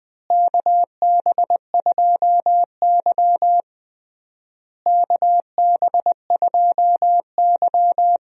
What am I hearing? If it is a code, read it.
Morse code, heard as a keyed tone, about 20 wpm, 700 Hz. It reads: KB2Y KB2Y
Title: USA Call Signs - 4 Characters - Repeated 3x - 20wpm